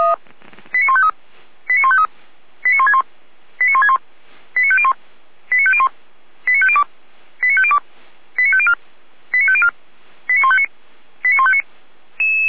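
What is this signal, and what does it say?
Five-tone selective call (ZVEI), heard as quick runs of five tones, tones of about 80 ms. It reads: 89153 89153 89161 89161 89501 89501 89502 89502 89504 89504 89159 89159
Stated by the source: ZVEI five-tone selective call received on SDR, from the feuerwehr-orlamuende.org fire brigade server.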